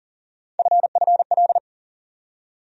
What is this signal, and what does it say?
Morse code, heard as a keyed tone, about 40 wpm, 700 Hz. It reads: FFL
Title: USA Call Sign Postfixes - 3 Letters 40wpm